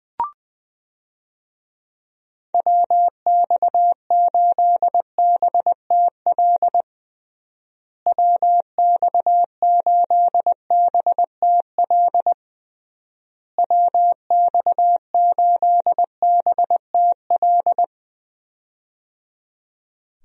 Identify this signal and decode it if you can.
Morse code, heard as a keyed tone, about 21 wpm, 700 Hz. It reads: WX8BTL WX8BTL WX8BTL
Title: USA Call Signs - 6 Characters - Repeated 3x - 20wpm